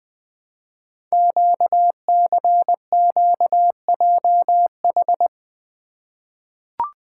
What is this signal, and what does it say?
Morse code, heard as a keyed tone, about 20 wpm, 700 Hz. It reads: QCQJH
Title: Code Groups - 5 Letters - 20wpm